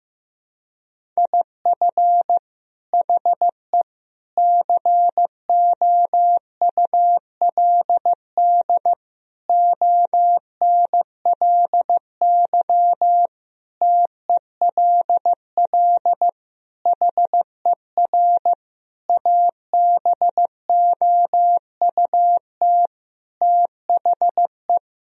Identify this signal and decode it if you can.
Morse code, heard as a keyed tone, about 15 wpm, 700 Hz. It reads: IF HE COULD ONLY TELL HER ABOUT THE
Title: Sentences from Top 1000 Words - Review Part 1 - 15wpm